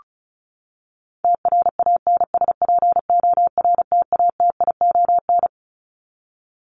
Morse code, heard as a keyed tone, about 34 wpm, 700 Hz. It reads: TRANSPORTATION
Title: Top 1501-1750 Words 35wpm